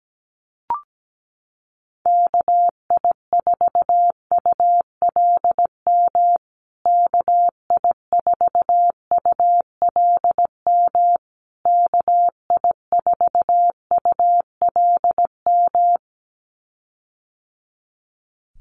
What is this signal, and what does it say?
Morse code, heard as a keyed tone, about 17 wpm, 700 Hz. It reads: KI4ULM KI4ULM KI4ULM
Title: USA Call Signs Repeated 3x 17wpm